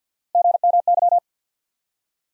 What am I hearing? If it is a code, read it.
Morse code, heard as a keyed tone, about 54 wpm, 700 Hz. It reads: GMY
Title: USA Call Sign Postfixes - 3 Letters 50wpm